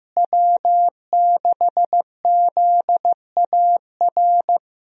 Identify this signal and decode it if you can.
Morse code, heard as a keyed tone, about 15 wpm, 700 Hz. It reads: W6ZAR
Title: USA Call Signs - 5 Characters - Repeated 2x - 15wpm